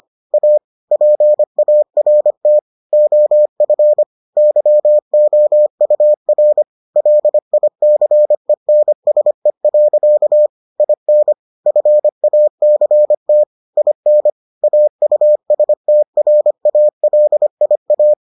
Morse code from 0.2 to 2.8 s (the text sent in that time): A PART